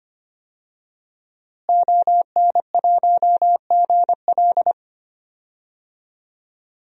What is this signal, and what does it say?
Morse code, heard as a keyed tone, about 25 wpm, 700 Hz. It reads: ON1GL
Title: DXCC-100 Call Signs - 5 Characters - Repeated 2x - 25wpm